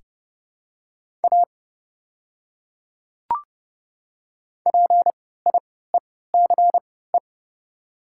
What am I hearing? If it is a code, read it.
Morse code, heard as a keyed tone, about 30 wpm, spaced as for 17 wpm, 700 Hz. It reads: A PIECE
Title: 3x Character Spacing - Top 500 Words - 30wpm